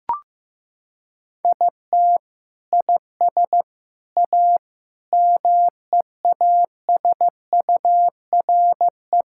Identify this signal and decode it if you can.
Morse code, heard as a keyed tone, about 16 wpm, 700 Hz. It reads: IT IS A MEASURE
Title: Sentences from Top 1600 Words 15wpm